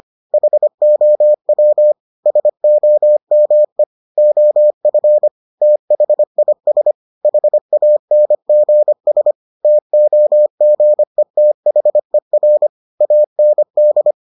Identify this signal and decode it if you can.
Morse code, heard as a keyed tone, about 25 wpm, 600 Hz. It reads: HOW SOME OF THIS HANGS TOGETHER AND